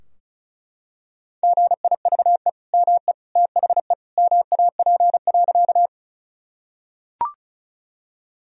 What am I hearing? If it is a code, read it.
Morse code, heard as a keyed tone, about 36 wpm, 700 Hz. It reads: GIVE ME THE MAP.
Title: Sentences from Top 1000 Words - Review Part 2 - 35wpm